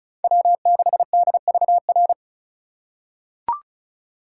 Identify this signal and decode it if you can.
Morse code, heard as a keyed tone, about 36 wpm, 700 Hz. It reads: W6DVR
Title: USA Call Signs - 5 Characters - 4x Character Spacing - 35wpm